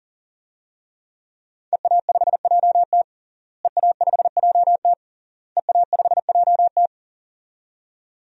Morse code, heard as a keyed tone, about 40 wpm, 700 Hz. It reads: EA5JT EA5JT EA5JT
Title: DXCC-100 Call Signs - 5 Characters - Repeated 3x - 40wpm